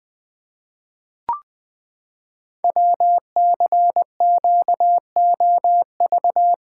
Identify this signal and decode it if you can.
Morse code, heard as a keyed tone, about 20 wpm, 700 Hz. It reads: WCQOV
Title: Code Groups - 5 Letters - 20wpm